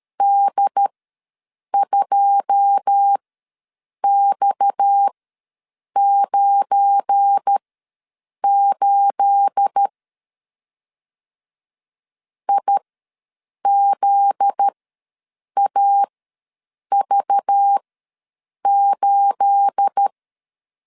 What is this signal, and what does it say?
Morse code, heard as a keyed tone, about 13 wpm, spaced as for 7 wpm, 800 Hz. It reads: D2X98 IZAV8
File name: LEZIONI / 9 28 ws 12wpm 3354